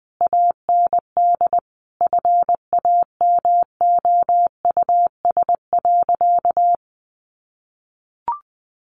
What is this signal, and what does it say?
Morse code, heard as a keyed tone, about 20 wpm, 700 Hz. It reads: AND FAMOUS.
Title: Sentences from Top 1500 Words 20wpm